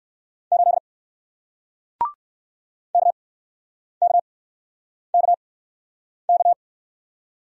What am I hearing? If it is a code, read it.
Morse code, heard as a keyed tone, about 61 wpm, spaced as for 9 wpm, 700 Hz. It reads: 8 KKKK